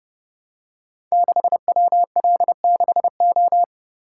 Morse code, heard as a keyed tone, about 30 wpm, 700 Hz. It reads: 6WL6O
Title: Code Groups - 5 Letter:Numbers - 30wpm